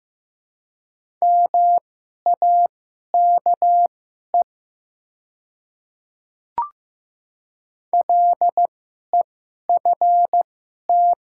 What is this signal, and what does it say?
Morse code, heard as a keyed tone, about 15 wpm, spaced as for 11 wpm, 700 Hz. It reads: MAKE LEFT
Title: 2x Character Spacing - Top 300 Words - 15wpm